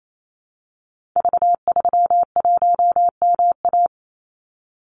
Morse code, heard as a keyed tone, about 28 wpm, 700 Hz. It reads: V31MA
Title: Call Signs - 28wpm